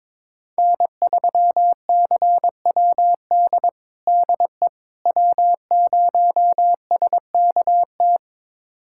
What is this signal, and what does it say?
Morse code, heard as a keyed tone, about 21 wpm, 700 Hz. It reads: N3CWD DE W0SKT
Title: USA Callsign DE Callsign - 5 Characters - 22wpm